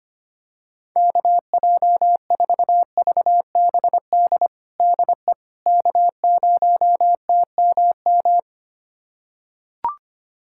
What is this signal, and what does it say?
Morse code, heard as a keyed tone, about 25 wpm, 700 Hz. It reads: KJ4VBD DE K0TMM
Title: USA Callsign DE Callsign 25wpm